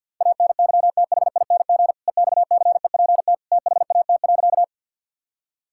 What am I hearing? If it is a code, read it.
Morse code, heard as a keyed tone, about 50 wpm, 700 Hz. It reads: ANYTHING EXCEPT THAT.
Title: Sentences from Top 1600 Words 50wpm